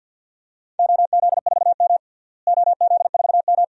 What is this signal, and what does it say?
Morse code, heard as a keyed tone, about 50 wpm, 700 Hz. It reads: OZ4M OZ4M
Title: DXCC-100 Call Signs - 4 Characters - Repeated 2x - 50wpm